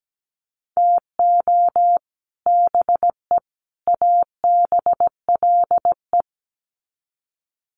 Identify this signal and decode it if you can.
Morse code, heard as a keyed tone, about 17 wpm, 700 Hz. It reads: TO BE ABLE
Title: Sets of 3 Words 17wpm